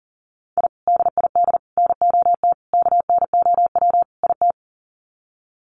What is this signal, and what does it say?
Morse code, heard as a keyed tone, about 40 wpm, 700 Hz. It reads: I DID NOT KNOW IT
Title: Sets of 5 Words Encore2 40wpm